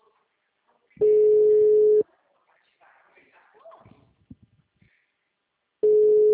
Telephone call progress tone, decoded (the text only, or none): ringback tone